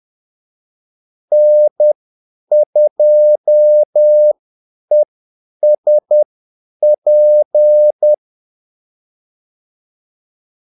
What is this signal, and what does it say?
Morse code, heard as a keyed tone, about 10 wpm, spaced as for 8 wpm, 600 Hz. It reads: N2ESP
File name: Callsigns-10-8wpm